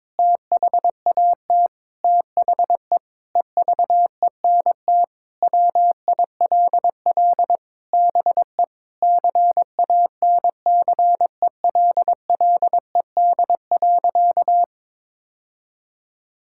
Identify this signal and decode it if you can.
Morse code, heard as a keyed tone, about 22 wpm, 700 Hz. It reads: THAT THE EVENT WILL BE CANCELLED.
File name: practice / 22wpm / Sentences with 8-Word Phrases - 22wpm